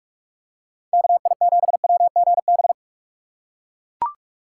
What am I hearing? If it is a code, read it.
Morse code, heard as a keyed tone, about 40 wpm, 700 Hz. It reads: KI7WGB